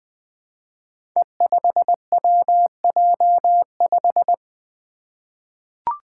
Morse code, heard as a keyed tone, about 20 wpm, 700 Hz. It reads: E5WJ5
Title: Five Challenging Letter-Number Combinations 20wpm